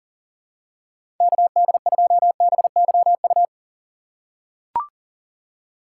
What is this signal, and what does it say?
Morse code, heard as a keyed tone, about 40 wpm, 700 Hz. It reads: KD2BYU
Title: USA Call Signs - 4x Character Spacing - 40wpm